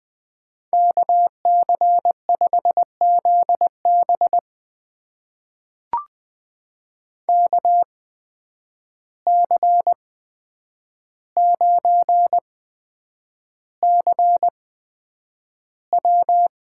Morse code, heard as a keyed tone, about 20 wpm, spaced as for 9 wpm, 700 Hz. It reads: KC5ZB KC9CW
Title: USA Call Signs - 5 Characters - 8x Character Spacing - 20wpm